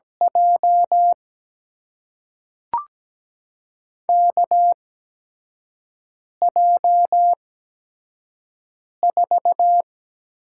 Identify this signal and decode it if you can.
Morse code, heard as a keyed tone, about 17 wpm, spaced as for 5 wpm, 700 Hz. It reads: J KJ4
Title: USA Call Signs - 6 Characters - 8x Character Spacing - 17wpm